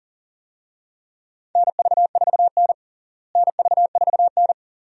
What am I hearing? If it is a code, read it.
Morse code, heard as a keyed tone, about 40 wpm, 700 Hz. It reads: NV4N NV4N
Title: USA Call Signs - 4 Characters - Repeated 2x - 40wpm